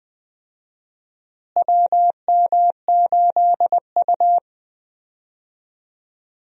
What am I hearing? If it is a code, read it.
Morse code, heard as a keyed tone, about 21 wpm, 700 Hz. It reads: WM8U
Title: USA Call Signs - 4 Characters - Repeated 3x - 20wpm